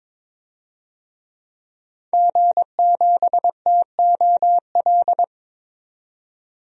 Morse code, heard as a keyed tone, about 22 wpm, 700 Hz. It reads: G7TOL
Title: DXCC-100 Call Signs - 5 Characters - 22wpm